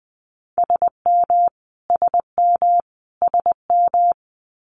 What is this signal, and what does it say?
Morse code, heard as a keyed tone, about 20 wpm, 700 Hz. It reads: SM SM SM
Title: Multi-letter Phonemes Repeated 3x 20wpm